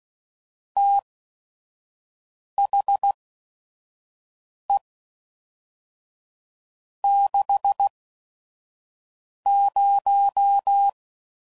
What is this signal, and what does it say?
Morse code, heard as a keyed tone, about 16 wpm, spaced as for 5 wpm, 800 Hz. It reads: THE 60
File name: CW Code Groups @5WPM